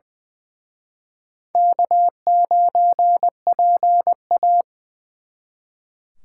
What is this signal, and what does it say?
Morse code, heard as a keyed tone, about 23 wpm, 700 Hz. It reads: K9PA
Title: Speed Racing - USA 1x2 Call Signs 20wpm